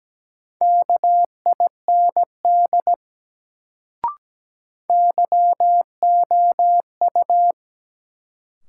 Morse code, heard as a keyed tone, about 17 wpm, 700 Hz. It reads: KIND YOU
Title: Sets of 2 Words - New Word Drill - 17wpm